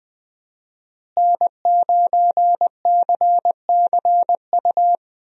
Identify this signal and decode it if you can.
Morse code, heard as a keyed tone, about 21 wpm, 700 Hz. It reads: N9CCU
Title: USA Call Signs - 5 Characters - Repeated 3x - 20wpm